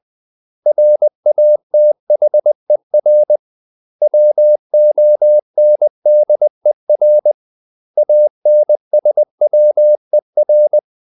Morse code, heard as a keyed tone, about 20 wpm, 600 Hz. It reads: RATHER WONDER ANSWER
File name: CW / Practice / 500MostCommonEnglishWords20